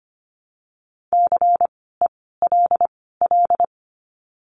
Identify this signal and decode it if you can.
Morse code, heard as a keyed tone, about 25 wpm, spaced as for 16 wpm, 700 Hz. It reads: CELL
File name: course / 25:10wpm / 094 Lesson 16f - Letter c - Words 25wpm